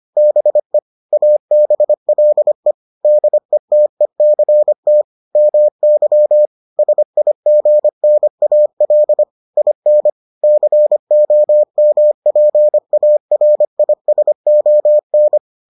Morse code, heard as a keyed tone, about 25 wpm, 600 Hz. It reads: BE ABLE DETECT MY SIGNAL IN COMPARISON